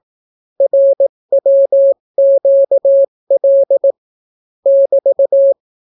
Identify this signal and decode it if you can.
Morse code, heard as a keyed tone, about 18 wpm, 550 Hz. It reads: RWQL <BT>